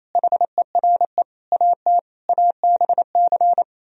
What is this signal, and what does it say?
Morse code, heard as a keyed tone, about 36 wpm, 700 Hz. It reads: HERE AT ABC